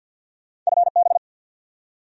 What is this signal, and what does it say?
Morse code, heard as a keyed tone, about 69 wpm, spaced as for 59 wpm, 700 Hz. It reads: FB
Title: Speed-Racing Abbreviations - 50wpm